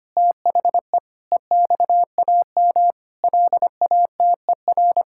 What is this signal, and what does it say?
Morse code, heard as a keyed tone, about 25 wpm, 700 Hz. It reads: THE EXAM LATER
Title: Sentences from Top 1000 Words - Review Part 5 - 25wpm